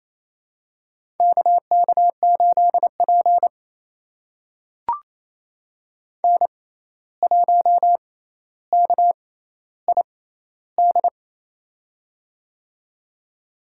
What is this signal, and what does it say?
Morse code, heard as a keyed tone, about 28 wpm, spaced as for 13 wpm, 700 Hz. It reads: KK8P N1KID
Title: USA Call Signs - 6x Character Spacing - 28wpm